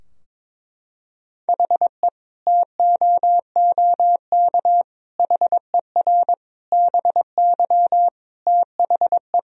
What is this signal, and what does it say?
Morse code, heard as a keyed tone, about 21 wpm, 700 Hz. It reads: HE TOOK HER BY THE
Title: Sentences from Top 1000 Words - Review Part 2 - 22wpm